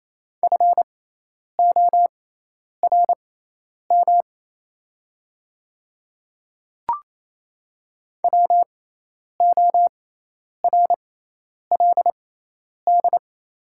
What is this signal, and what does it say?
Morse code, heard as a keyed tone, about 28 wpm, spaced as for 10 wpm, 700 Hz. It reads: FORM WORLD
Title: 6x Character Spacing - Top 300 Words - 28wpm